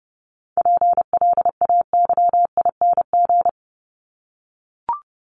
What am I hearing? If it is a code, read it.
Morse code, heard as a keyed tone, about 30 wpm, 700 Hz. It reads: PLAYING
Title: ING Suffix - Top 100 Words with Base Words 30wpm